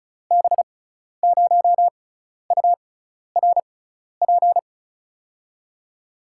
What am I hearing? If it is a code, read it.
Morse code, heard as a keyed tone, about 36 wpm, spaced as for 12 wpm, 700 Hz. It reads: B0URP
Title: USA Call Signs - 6 Characters - 6x Character Spacing - 35wpm